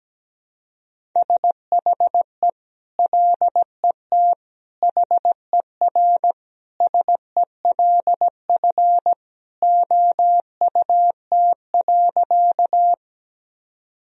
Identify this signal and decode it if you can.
Morse code, heard as a keyed tone, about 17 wpm, 700 Hz. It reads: SHE LET HERSELF OUT.
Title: Sentences from Top 1000 Words - Review Part 1 - 17wpm